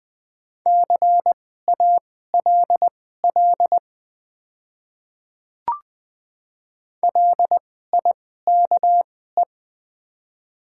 Morse code, heard as a keyed tone, about 20 wpm, spaced as for 14 wpm, 700 Hz. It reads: CALL LIKE
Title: 2x Character Spacing - Top 200 Words - 20wpm